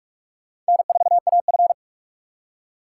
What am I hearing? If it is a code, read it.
Morse code, heard as a keyed tone, about 59 wpm, 700 Hz. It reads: N4AF